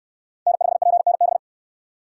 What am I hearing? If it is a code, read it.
Morse code, heard as a keyed tone, about 69 wpm, 700 Hz. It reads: N5PNB